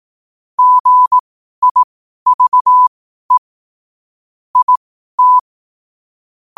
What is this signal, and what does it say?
Morse code, heard as a keyed tone, about 18 wpm, spaced as for 12 wpm, 1000 Hz. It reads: GIVE IT
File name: qotdfiles / 12wpm